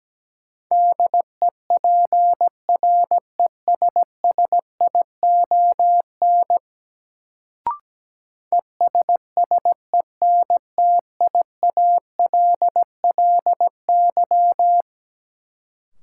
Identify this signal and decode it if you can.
Morse code, heard as a keyed tone, about 17 wpm, 700 Hz. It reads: DEPRESSION ESSENTIALLY
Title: Sentences from Top 1300 Words - New Word Drill Alt - 17wpm